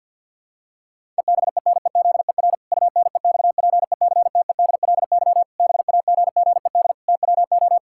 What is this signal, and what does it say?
Morse code, heard as a keyed tone, about 50 wpm, 700 Hz. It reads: EBENEZER UNEXPECTEDLY BAGGED TWO
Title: Pangram Challenge v2 - 50wpm